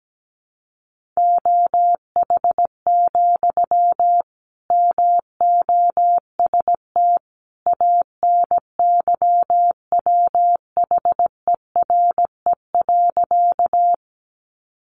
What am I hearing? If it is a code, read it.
Morse code, heard as a keyed tone, about 17 wpm, 700 Hz. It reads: OH, MOST ANYWHERE.
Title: Sentences from Top 1400 Words 17wpm